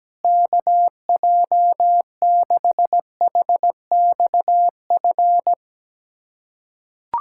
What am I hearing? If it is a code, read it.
Morse code, heard as a keyed tone, about 17 wpm, 700 Hz. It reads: KJ6HXF
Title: USA Callsign DE Callsign 17wpm